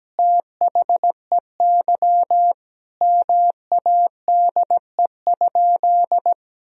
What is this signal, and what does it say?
Morse code, heard as a keyed tone, about 17 wpm, 700 Hz. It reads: THEY MADE?
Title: Sentences from Top 1000 Words - Review Part 3 - 17wpm